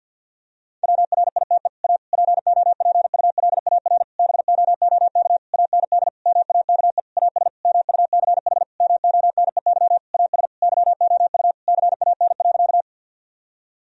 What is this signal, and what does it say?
Morse code, heard as a keyed tone, about 50 wpm, 700 Hz. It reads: WRITE A POPULAR BOOK AND MAKE AS MUCH MONEY AS YOU CAN.
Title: Sentences from Top 1000 Words - Review Part 3 - 50wpm